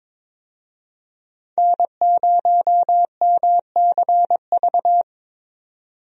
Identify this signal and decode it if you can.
Morse code, heard as a keyed tone, about 21 wpm, 700 Hz. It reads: N0MCV
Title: USA Call Signs - 5 Characters - 22wpm